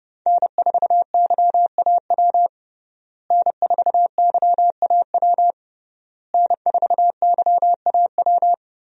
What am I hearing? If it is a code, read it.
Morse code, heard as a keyed tone, about 30 wpm, 700 Hz. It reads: N4YAW N4YAW N4YAW
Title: USA Call Signs - 5 Characters - Repeated 3x - 30wpm